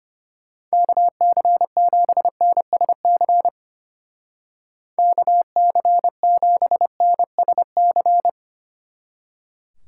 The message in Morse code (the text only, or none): KC7NSC KC7NSC